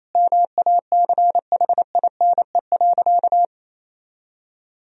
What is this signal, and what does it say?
Morse code, heard as a keyed tone, about 28 wpm, 700 Hz. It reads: MACHINE.
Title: Sentences from Top 1000 Words - Review Part 2 - 28wpm